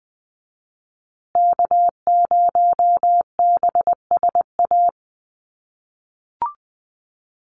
Morse code, heard as a keyed tone, about 20 wpm, 700 Hz. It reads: K0BSA
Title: USA Call Signs Repeated 3x 20wpm